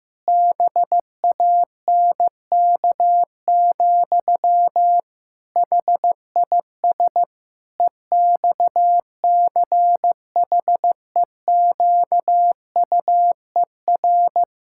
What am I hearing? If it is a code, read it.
Morse code, heard as a keyed tone, about 15 wpm, 700 Hz. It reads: BANK, HIS EXCHEQUER
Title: Pangram Challenge v2 - 15wpm